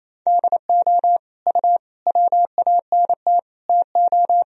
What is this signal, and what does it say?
Morse code, heard as a keyed tone, about 28 wpm, 700 Hz. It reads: DO U WANT TO
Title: Sentences from Top 300 Words and Abbreviations 28wpm